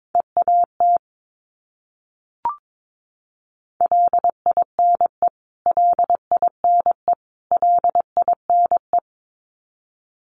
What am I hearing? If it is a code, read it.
Morse code, heard as a keyed tone, about 22 wpm, 700 Hz. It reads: EAT LINE LINE LINE
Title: Top 500 Words Repeated 22wpm